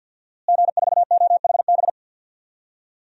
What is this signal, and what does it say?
Morse code, heard as a keyed tone, about 48 wpm, 700 Hz. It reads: G4OHB
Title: DXCC-100 Call Signs - 5 Characters - Repeated 2x - 50wpm